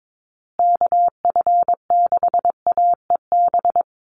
Morse code, heard as a keyed tone, about 22 wpm, 700 Hz. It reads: KF6AEB
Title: USA Call Signs Repeated 3x 22wpm